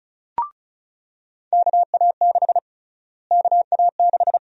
Morse code, heard as a keyed tone, about 36 wpm, 700 Hz. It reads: KA6 KA6
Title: USA Call Sign Prefixes - Repeated 2x - 35wpm